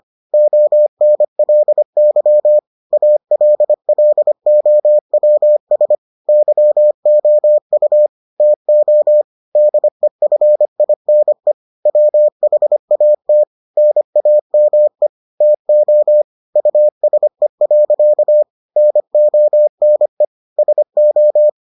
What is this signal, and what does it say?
Morse code, heard as a keyed tone, about 25 wpm, 600 Hz. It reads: ONLY ALLOWS YOU TO DEFINE WHAT NAME TO USE. NONE SO